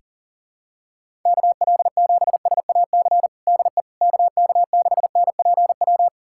Morse code, heard as a keyed tone, about 40 wpm, 700 Hz. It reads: KL7SAC DE KK6NPW
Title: USA Callsign DE Callsign - 6 Characters - 40wpm